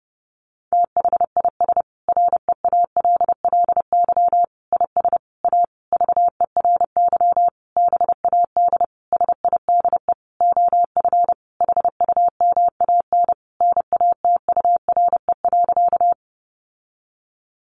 Morse code, heard as a keyed tone, about 30 wpm, 700 Hz. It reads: THIS REALLY IS A VERY BAD SIDE OF HUMAN NATURE.
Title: Sentences from Top 700 Words 30wpm